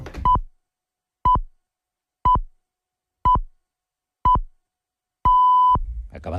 Radio time signal (time pips). It is heard from 0.2 to 5.8 s.